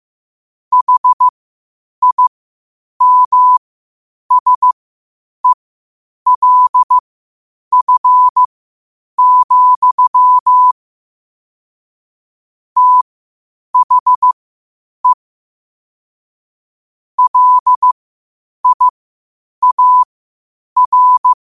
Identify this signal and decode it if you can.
Morse code, heard as a keyed tone, about 15 wpm, spaced as for 8 wpm, 1000 Hz. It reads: HIMSELF, THE LIAR
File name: qotdfiles / 8wpm